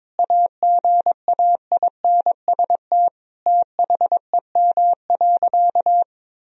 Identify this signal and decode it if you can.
Morse code, heard as a keyed tone, about 22 wpm, 700 Hz. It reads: AGAINST THEM.
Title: Sentences from Top 1000 Words - Review Part 4 - 22wpm